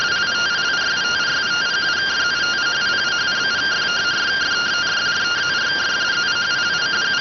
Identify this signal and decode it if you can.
Two-tone teleprinter signal, about 170 Hz shift.